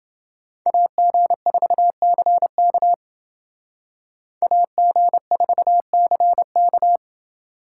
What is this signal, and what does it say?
Morse code, heard as a keyed tone, about 28 wpm, 700 Hz. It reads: AG4CK AG4CK